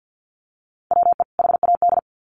Morse code, heard as a keyed tone, about 50 wpm, 700 Hz. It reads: WE HAD
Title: Sets of 2 Words 50wpm